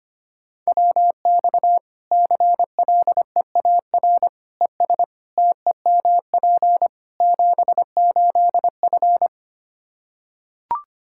Morse code, heard as a keyed tone, about 25 wpm, 700 Hz. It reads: WX CLEAR ES TEMP 78F
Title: Practice QSO - Exchange 2 - Info - 25wpm